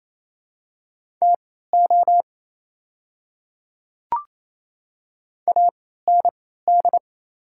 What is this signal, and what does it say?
Morse code, heard as a keyed tone, about 27 wpm, spaced as for 16 wpm, 690 Hz. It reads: TO AND
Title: 3x Character Spacing - Top 200 Words - 28wpm